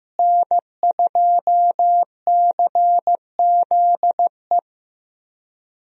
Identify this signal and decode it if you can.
Morse code, heard as a keyed tone, about 15 wpm, 700 Hz. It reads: N2CZE
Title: USA Call Signs - 5 Characters - Repeated 2x - 15wpm